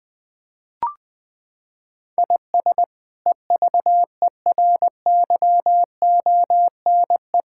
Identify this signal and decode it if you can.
Morse code, heard as a keyed tone, about 20 wpm, 700 Hz. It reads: IS EVERYONE
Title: Sentences from Top 1000 Words - Review Part 5 - 20wpm